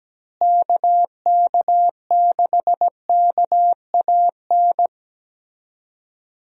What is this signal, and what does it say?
Morse code, heard as a keyed tone, about 17 wpm, 700 Hz. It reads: KK6KAN
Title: USA Call Signs - 6 Characters - Repeated 2x - 17wpm